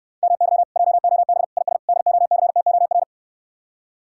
Morse code, heard as a keyed tone, about 69 wpm, 700 Hz. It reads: MY JOB IS DOCTOR